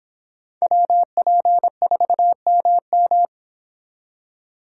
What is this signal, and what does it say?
Morse code, heard as a keyed tone, about 26 wpm, 700 Hz. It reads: WP4MM